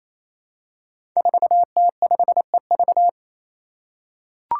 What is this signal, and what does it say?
Morse code, heard as a keyed tone, about 28 wpm, 700 Hz. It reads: 4T5EV
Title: Five Challenging Letter-Number Combinations 28wpm